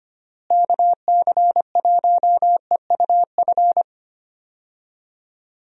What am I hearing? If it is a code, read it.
Morse code, heard as a keyed tone, about 25 wpm, 700 Hz. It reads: KC1EUF
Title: USA Call Signs - 6 Characters - Repeated 2x - 25wpm